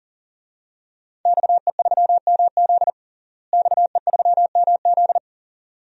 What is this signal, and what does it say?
Morse code, heard as a keyed tone, about 40 wpm, 700 Hz. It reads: XE3MZ XE3MZ
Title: DXCC-100 Call Signs - 5 Characters - Repeated 2x - 40wpm